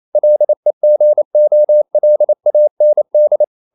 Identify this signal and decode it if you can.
Morse code, heard as a keyed tone, about 28 wpm, 600 Hz. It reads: LEGOLAND
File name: QRQ 28wpm